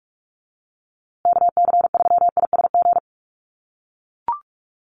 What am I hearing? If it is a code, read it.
Morse code, heard as a keyed tone, about 44 wpm, 690 Hz. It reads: KC3ISG
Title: USA Call Signs Repeated 3x 45wpm